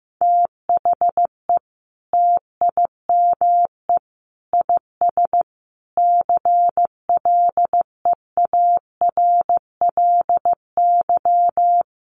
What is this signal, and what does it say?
Morse code, heard as a keyed tone, about 15 wpm, 700 Hz. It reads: THE TIME IS CLEARLY
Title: Sentences from Top 1300 Words 15wpm